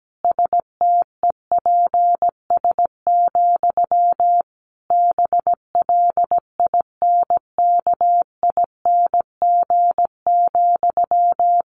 Morse code, heard as a keyed tone, about 17 wpm, 700 Hz. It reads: STEPS, BLINKING,
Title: A Sound Of Thunder 17wpm